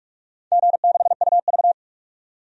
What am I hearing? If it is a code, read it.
Morse code, heard as a keyed tone, about 45 wpm, 700 Hz. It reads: G6UV
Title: DXCC-100 Call Signs - 4 Characters - 45wpm